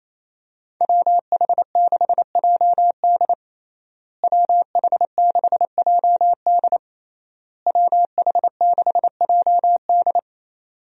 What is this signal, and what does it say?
Morse code, heard as a keyed tone, about 28 wpm, 700 Hz. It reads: WH6JD WH6JD WH6JD
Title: USA Call Signs - 5 Characters - Repeated 3x - 28wpm